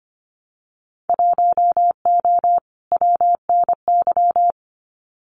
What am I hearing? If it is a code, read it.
Morse code, heard as a keyed tone, about 25 wpm, 700 Hz. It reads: 1O WNY
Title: Winter Field Day Exchanges 25wpm